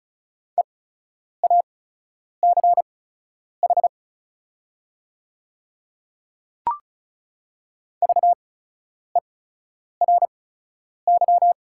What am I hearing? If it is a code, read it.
Morse code, heard as a keyed tone, about 36 wpm, spaced as for 10 wpm, 700 Hz. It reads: EACH VERY
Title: 8x Character Spacing - Top 200 Words - 35wpm